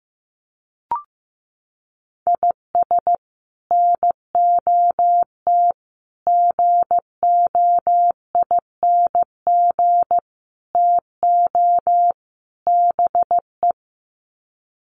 Morse code, heard as a keyed tone, about 15 wpm, 700 Hz. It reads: IS NOT GOING TO BE
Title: Sets of 5 Words 15wpm